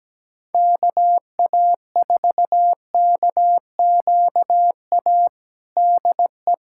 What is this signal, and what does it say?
Morse code, heard as a keyed tone, about 17 wpm, 700 Hz. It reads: KA4KQA DE
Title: USA Callsign DE Callsign 17wpm